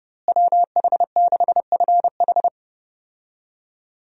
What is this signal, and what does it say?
Morse code, heard as a keyed tone, about 30 wpm, 700 Hz. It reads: WH6FH
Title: USA Call Signs - 5 Characters - Repeated 3x - 30wpm